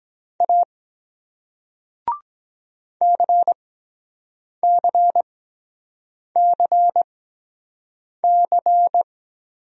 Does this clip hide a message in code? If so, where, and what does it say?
Yes: Morse code from 0.4 to 9.0 s, heard as a keyed tone, about 23 wpm, spaced as for 7 wpm, 700 Hz. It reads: A CCCC